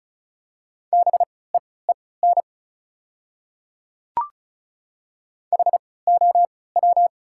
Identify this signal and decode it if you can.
Morse code, heard as a keyed tone, about 36 wpm, spaced as for 20 wpm, 700 Hz. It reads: BEEN HOW
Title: 3x Character Spacing - Top 300 Words - 35wpm